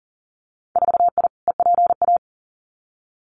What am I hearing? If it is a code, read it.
Morse code, heard as a keyed tone, about 42 wpm, 700 Hz. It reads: 4I EPA
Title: Winter Field Day Exchanges 40wpm